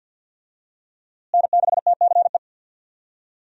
Morse code, heard as a keyed tone, about 52 wpm, 700 Hz. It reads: N6TCE
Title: USA Call Signs - 5 Characters - Repeated 3x - 50wpm